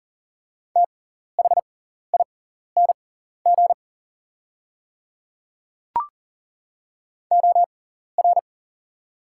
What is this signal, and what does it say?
Morse code, heard as a keyed tone, about 40 wpm, spaced as for 14 wpm, 700 Hz. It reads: THING OR
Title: 6x Character Spacing - Top 100 Words - 40wpm